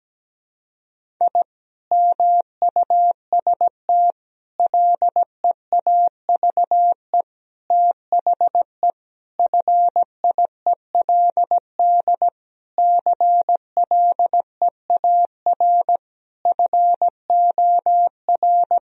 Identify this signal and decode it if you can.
Morse code, heard as a keyed tone, about 17 wpm, 700 Hz. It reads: I MUST LEAVE THE FIELD CLEAR FOR
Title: Sentences from Top 1000 Words - Review Part 2 - 17wpm